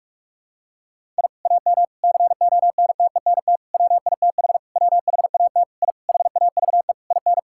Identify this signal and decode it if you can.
Morse code, heard as a keyed tone, about 41 wpm, 700 Hz. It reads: I AM CONTENT WITH WHAT I HAVE IN
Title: Sentences from Top 1000 Words - Review Part 6 - 45wpm